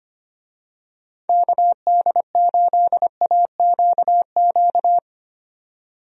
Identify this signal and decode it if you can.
Morse code, heard as a keyed tone, about 25 wpm, 700 Hz. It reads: KD8AQQ
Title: USA Call Signs - 6 Characters - 25wpm